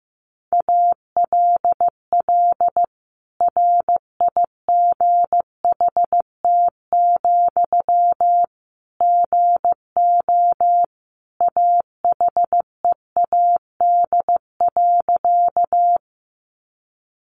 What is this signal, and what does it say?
Morse code, heard as a keyed tone, about 15 wpm, 700 Hz. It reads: ALL RIGHT, GO AHEAD.
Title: Sentences from Top 1100 Words 15wpm